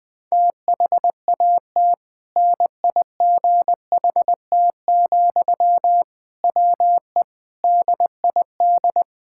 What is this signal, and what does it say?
Morse code, heard as a keyed tone, about 20 wpm, 700 Hz. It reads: THAT NIGHT, WE DID
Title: Sentences from Top 1000 Words - Review Part 6 - 20wpm